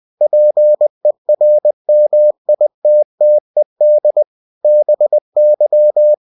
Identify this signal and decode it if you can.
Morse code, heard as a keyed tone, about 20 wpm, 600 Hz. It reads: PERMITTED BY
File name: cw-books / winnie-the-pooh / winnie-20wpm-page-0119